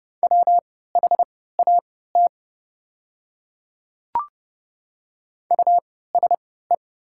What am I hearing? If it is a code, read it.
Morse code, heard as a keyed tone, about 30 wpm, spaced as for 17 wpm, 700 Hz. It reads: WHAT USE
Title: 3x Character Spacing - Top 100 Words - 30wpm